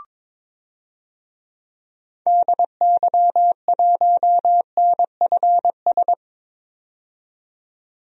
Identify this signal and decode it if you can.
Morse code, heard as a keyed tone, about 22 wpm, 700 Hz. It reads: DY1NFS
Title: DXCC-100 Call Signs - 6 Characters - 22wpm